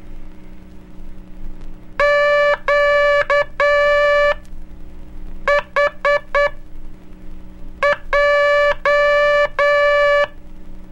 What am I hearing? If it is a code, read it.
Morse code, heard as a keyed tone, about 8 wpm, spaced as for 5 wpm, 580 Hz. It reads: QHJ
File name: cursos / Curso CW mp3 / CW Aula - 04